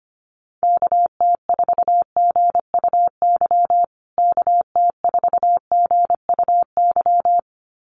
Morse code, heard as a keyed tone, about 25 wpm, 700 Hz. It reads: KT4GUY KT4GUY
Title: USA Call Signs Repeated 2x 25wpm